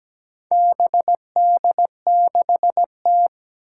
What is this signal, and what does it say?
Morse code, heard as a keyed tone, about 17 wpm, 700 Hz. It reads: BD6T
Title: Four Challenging Letter-Number Combinations 17wpm